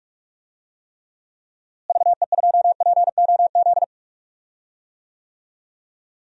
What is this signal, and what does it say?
Morse code, heard as a keyed tone, about 42 wpm, 700 Hz. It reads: VE2POZ
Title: DXCC-100 Call Signs - 6 Characters - Repeated 2x - 45wpm